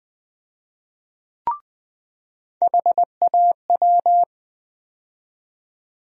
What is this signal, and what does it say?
Morse code, heard as a keyed tone, about 20 wpm, 700 Hz. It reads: HAW
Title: USA Call Sign Postfixes - 3 Letters 20wpm